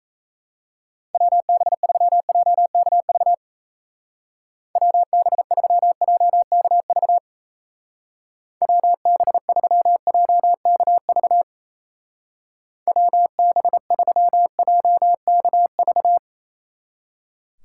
Morse code, heard as a keyed tone, about 34 wpm, 700 Hz. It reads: WB3JKV WB3JKV WB3JKV WB3JKV
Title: Speed Racing - USA 2x3 Call Signs 28wpm